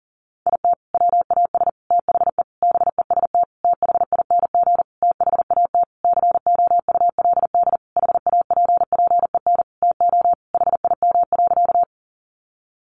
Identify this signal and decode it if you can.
Morse code, heard as a keyed tone, about 40 wpm, 700 Hz. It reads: IT WAS THE BEST THING THAT COULD HAPPEN TO HIM.
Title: Sentences from Top 400 Words 40wpm